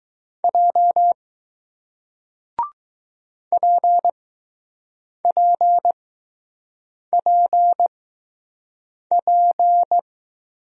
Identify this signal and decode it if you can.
Morse code, heard as a keyed tone, about 20 wpm, spaced as for 6 wpm, 700 Hz. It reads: J PPPP